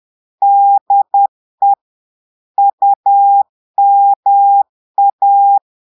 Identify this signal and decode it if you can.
Morse code, heard as a keyed tone, about 10 wpm, 800 Hz. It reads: DE UMA
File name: Portugais - texte en clair 1 - vitesse - 10 mots minute